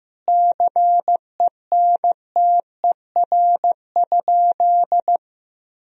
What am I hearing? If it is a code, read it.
Morse code, heard as a keyed tone, about 16 wpm, 700 Hz. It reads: CENTER?
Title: Sentences from Top 1000 Words - Review Part 2 - 15wpm